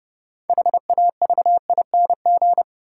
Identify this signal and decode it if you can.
Morse code, heard as a keyed tone, about 32 wpm, 700 Hz. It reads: HAVING
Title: Sets of 5 Words Encore - New Word Drill - 30wpm